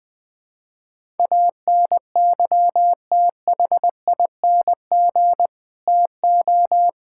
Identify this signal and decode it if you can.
Morse code, heard as a keyed tone, about 20 wpm, 700 Hz. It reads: ANYTHING TO
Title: Build A Phrase - 4-Word Phrases - 20wpm